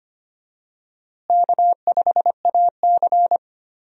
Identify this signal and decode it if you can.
Morse code, heard as a keyed tone, about 25 wpm, 700 Hz. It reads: K5AC
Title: USA Call Signs - 4 Characters - Repeated 2x - 25wpm